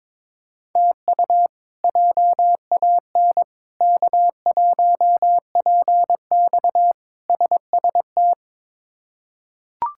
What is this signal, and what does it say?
Morse code, heard as a keyed tone, about 22 wpm, 700 Hz. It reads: TU JAN K1PX SST
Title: K1USN SST QSO Practice - 22wpm